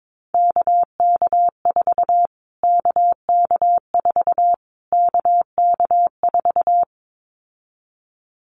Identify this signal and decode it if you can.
Morse code, heard as a keyed tone, about 22 wpm, 700 Hz. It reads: KK4 KK4 KK4
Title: Callsign Prefixes for Top 25 DXCC - Repeated 3x - 22wpm